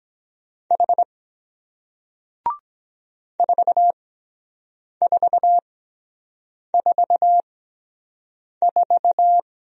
Morse code, heard as a keyed tone, about 23 wpm, spaced as for 7 wpm, 700 Hz. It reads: H 4444